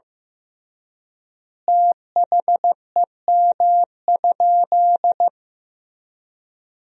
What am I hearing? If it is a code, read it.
Morse code, heard as a keyed tone, about 15 wpm, 700 Hz. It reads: THEM?
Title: 3x Word-Spacing - Sentences from Top 100 Words - 15wpm